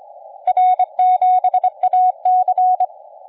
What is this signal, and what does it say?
Morse code, heard as a keyed tone, about 25 wpm, 740 Hz. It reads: R7AC
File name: la4c / sac-cw-2017 / CONTEST AUDIO 1100Z